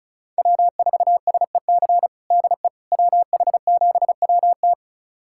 Morse code, heard as a keyed tone, about 34 wpm, 700 Hz. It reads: W4SEC DE WH7WT
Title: USA Callsign DE Callsign - 5 Characters - 35wpm